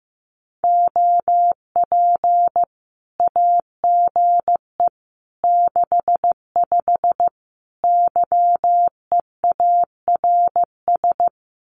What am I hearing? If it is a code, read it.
Morse code, heard as a keyed tone, about 15 wpm, 700 Hz. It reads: OP AGE 65 YEARS
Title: Common QSO Phrases 15wpm